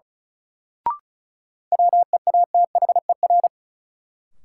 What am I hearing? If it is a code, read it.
Morse code, heard as a keyed tone, about 36 wpm, 700 Hz. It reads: WEATHER
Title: Sentences from Top 800 Words - New Word Drill - 35wpm